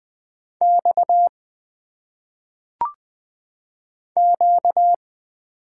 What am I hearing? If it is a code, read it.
Morse code, heard as a keyed tone, about 20 wpm, 700 Hz. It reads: X Q
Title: Single Letters - Phonetic Pronunciation 20wpm